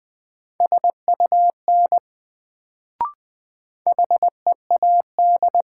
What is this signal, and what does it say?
Morse code, heard as a keyed tone, about 20 wpm, 700 Hz. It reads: SUN HEAD
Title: Sentences from Top 300 Words - New Word Drill - 20wpm